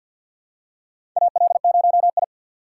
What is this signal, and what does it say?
Morse code, heard as a keyed tone, about 53 wpm, 700 Hz. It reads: AL0I
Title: USA Call Signs - 4 Characters - 6x Character Spacing - 50wpm